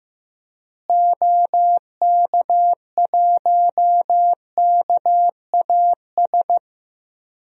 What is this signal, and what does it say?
Morse code, heard as a keyed tone, about 15 wpm, 700 Hz. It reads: OK1KAS
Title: DXCC-100 Call Signs - 6 Characters - Repeated 3x - 15wpm